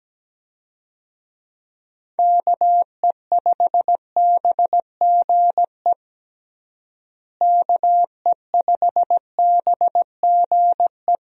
Morse code, heard as a keyed tone, about 17 wpm, 700 Hz. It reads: KE5BGE KE5BGE
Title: USA Call Signs - 6 Characters - Repeated 2x - 17wpm